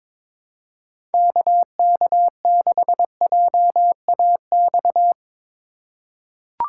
Morse code, heard as a keyed tone, about 22 wpm, 700 Hz. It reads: KK6JAX
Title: USA Call Signs - 6 Characters - 4x Character Spacing - 22wpm